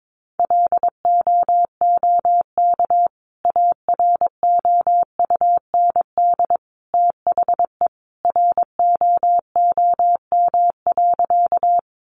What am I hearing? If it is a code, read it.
Morse code, heard as a keyed tone, about 22 wpm, 700 Hz. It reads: LOOK AROUND THE ROOM.
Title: Sentences from Top 600 Words 22wpm